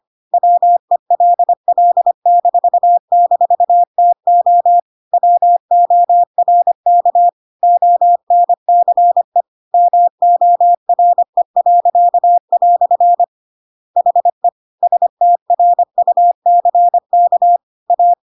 Morse code, heard as a keyed tone, about 25 wpm, 700 Hz. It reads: WELL--TO WORK ONCE MORE." HE STRUCK A
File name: Pinocchio / Pinocchio ch1 25 25 ik1hge